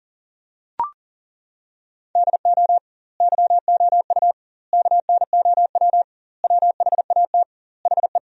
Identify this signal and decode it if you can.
Morse code, heard as a keyed tone, about 40 wpm, 700 Hz. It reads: DO YOU KNOW WHAT HE
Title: 2x Word-Spacing - Sentences from Top 300 Words - 40wpm